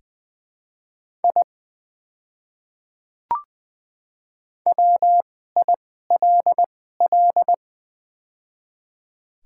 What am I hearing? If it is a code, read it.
Morse code, heard as a keyed tone, about 20 wpm, spaced as for 14 wpm, 700 Hz. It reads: I WILL
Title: 2x Character Spacing - Top 200 Words - 20wpm